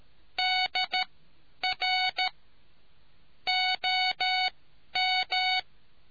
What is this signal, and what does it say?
Morse code, heard as a keyed tone, about 13 wpm, spaced as for 10 wpm, 740 Hz. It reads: DR OM